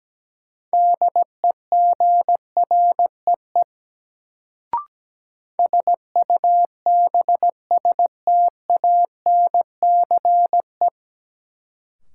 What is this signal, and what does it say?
Morse code, heard as a keyed tone, about 17 wpm, 700 Hz. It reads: DEGREE SUBSTANCE
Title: Sentences from Top 1500 Words - New Word Drill - 17wpm